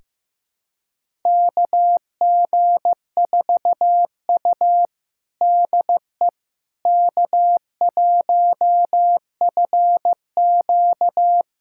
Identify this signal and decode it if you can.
Morse code, heard as a keyed tone, about 15 wpm, 700 Hz. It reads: KG4U DE K1FQ
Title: USA Callsign DE Callsign - 4 Characters - 15wpm